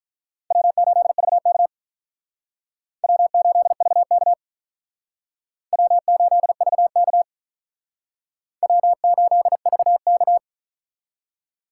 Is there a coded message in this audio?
Yes: Morse code from 0.5 to 10.4 s, heard as a keyed tone, about 44 wpm, 700 Hz. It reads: W8VK W8VK W8VK W8VK